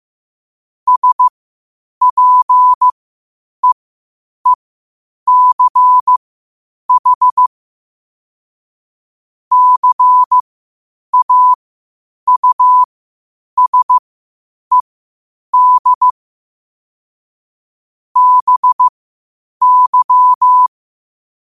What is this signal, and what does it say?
Morse code, heard as a keyed tone, about 15 wpm, spaced as for 8 wpm, 1000 Hz. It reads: SPEECH CAUSED BY
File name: qotdfiles / 8wpm